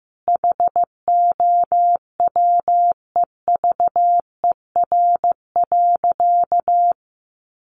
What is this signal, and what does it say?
Morse code, heard as a keyed tone, about 15 wpm, 700 Hz. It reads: HOWEVER.
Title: Sentences from Top 600 Words 15wpm